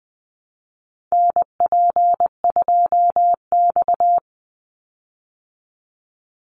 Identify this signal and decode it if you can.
Morse code, heard as a keyed tone, about 20 wpm, 700 Hz. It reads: NP2X
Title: Call Signs - 20wpm